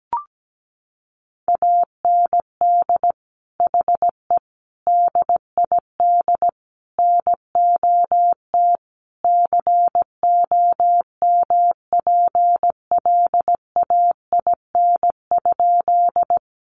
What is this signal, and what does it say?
Morse code, heard as a keyed tone, about 17 wpm, 700 Hz. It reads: AND HE DID NOT COMPLAIN?
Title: Sentences from Top 1300 Words 17wpm